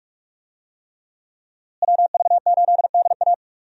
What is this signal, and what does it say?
Morse code, heard as a keyed tone, about 41 wpm, 700 Hz. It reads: WV8DA
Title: USA Call Signs - 5 Characters - Repeated 3x - 45wpm